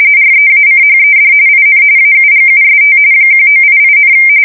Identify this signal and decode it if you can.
RTTY, about 45.45 baud. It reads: REACHED IN AUGUST 1988,
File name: fsk170 / battleshipnj